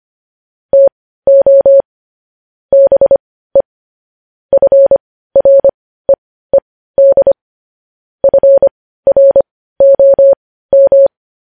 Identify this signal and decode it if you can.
Morse code, heard as a keyed tone, about 25 wpm, spaced as for 15 wpm, 550 Hz. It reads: TO BE FREED FROM
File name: podcast / News headlines-1x-25x15